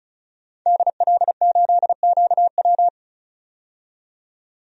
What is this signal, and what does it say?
Morse code, heard as a keyed tone, about 34 wpm, 700 Hz. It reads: DL8QW
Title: DXCC-100 Call Signs - 5 Characters - 35wpm